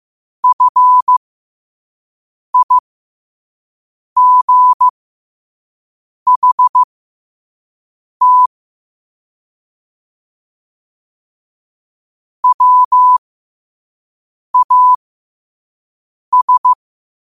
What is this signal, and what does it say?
Morse code, heard as a keyed tone, about 15 wpm, spaced as for 5 wpm, 1000 Hz. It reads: FIGHT WAS
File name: qotdfiles / 5wpm